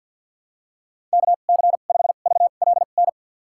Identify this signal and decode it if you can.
Morse code, heard as a keyed tone, about 49 wpm, spaced as for 35 wpm, 700 Hz. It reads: KC5VLN
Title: USA Call Signs - 2x Character Spacing - 50wpm